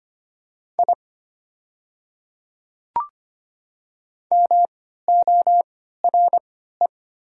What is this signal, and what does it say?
Morse code, heard as a keyed tone, about 25 wpm, spaced as for 14 wpm, 700 Hz. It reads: I MORE
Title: 3x Character Spacing - Top 100 Words - 25wpm